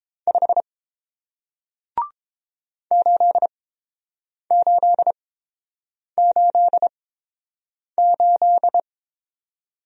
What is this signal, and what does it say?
Morse code, heard as a keyed tone, about 30 wpm, spaced as for 7 wpm, 700 Hz. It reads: 5 8888